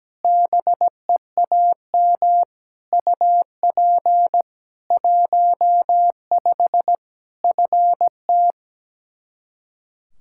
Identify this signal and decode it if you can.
Morse code, heard as a keyed tone, about 17 wpm, 700 Hz. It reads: BEAM UP 15 FT
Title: Practice QSO - Exchange 2 - Info - 17wpm